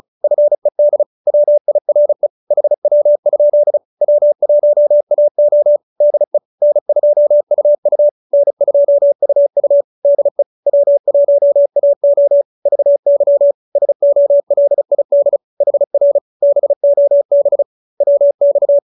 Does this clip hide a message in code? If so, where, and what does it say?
Yes: Morse code from 0.2 to 18.8 s, heard as a keyed tone, about 34 wpm, 570 Hz. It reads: FED WIRE HW? W1AO DE N2UU N2UU DE W1AO VY SOLID HR BOB WX